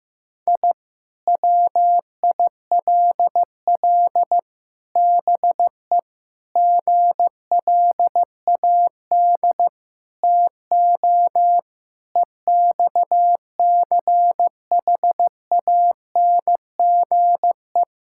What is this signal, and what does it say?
Morse code, heard as a keyed tone, about 16 wpm, 700 Hz. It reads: I WILL BE GLAD TO EXCHANGE
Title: Sentences from Top 1000 Words - Review Part 6 - 15wpm